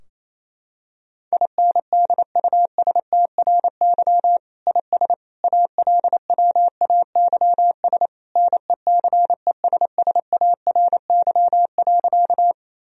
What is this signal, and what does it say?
Morse code, heard as a keyed tone, about 28 wpm, 700 Hz. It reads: INDUSTRY IS ALWAYS NECESSARY.
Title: Sentences from Top 1000 Words - Review Part 4 - 28wpm